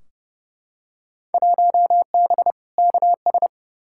Morse code, heard as a keyed tone, about 30 wpm, 700 Hz. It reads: 1B KS
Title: Field Day Exchanges Encore - 30wpm